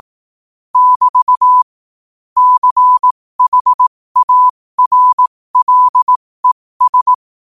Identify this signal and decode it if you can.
Morse code, heard as a keyed tone, about 18 wpm, spaced as for 15 wpm, 1000 Hz. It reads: <BT> CHARLES
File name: qotdfiles / 15wpm